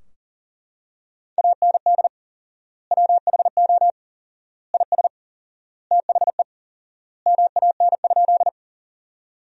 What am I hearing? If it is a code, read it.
Morse code, heard as a keyed tone, about 37 wpm, 700 Hz. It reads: AND WHO IS THE MAN?
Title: 4x Word-Spacing - Sentences from Top 200 Words - 40wpm